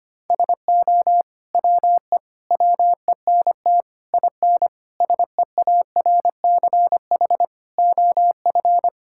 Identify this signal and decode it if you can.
Morse code, heard as a keyed tone, about 25 wpm, 700 Hz. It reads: SO WE WENT IN SEARCH OF
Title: Sentences from Top 1000 Words - Review Part 5 - 25wpm